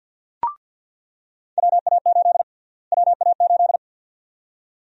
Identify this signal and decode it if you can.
Morse code, heard as a keyed tone, about 48 wpm, 700 Hz. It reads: WA8 WA8
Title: USA Call Sign Prefixes - Repeated 2x - 50wpm